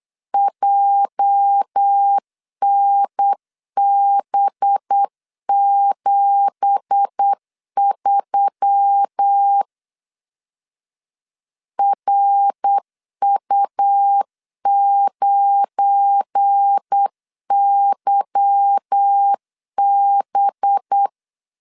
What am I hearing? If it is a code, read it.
Morse code, heard as a keyed tone, about 8 wpm, 800 Hz. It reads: JNB73 RU9YB
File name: LEZIONI / 10 SIGNS / 10A SLASH 8wpm 3452